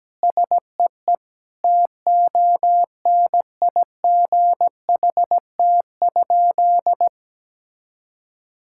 Morse code, heard as a keyed tone, about 17 wpm, 700 Hz. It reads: SEE TONIGHT?
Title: Sentences from Top 1000 Words - Review Part 5 - 17wpm